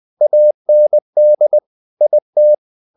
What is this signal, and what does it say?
Morse code, heard as a keyed tone, about 20 wpm, 600 Hz. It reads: AND IT
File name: the old man and the sea / the old man and the sea part 1 20wpm